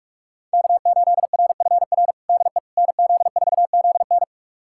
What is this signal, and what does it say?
Morse code, heard as a keyed tone, about 46 wpm, 700 Hz. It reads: K8RFR DE NZ4ZN
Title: USA Callsign DE Callsign - 5 Characters - 45wpm